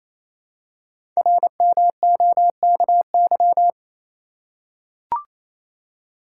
Morse code, heard as a keyed tone, about 29 wpm, 700 Hz. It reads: RMOKY
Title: Code Groups - 5 Letters - 28wpm